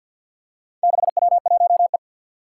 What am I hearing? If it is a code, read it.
Morse code, heard as a keyed tone, about 50 wpm, 700 Hz. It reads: 6W1E
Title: Four Challenging Letter-Number Combinations 50wpm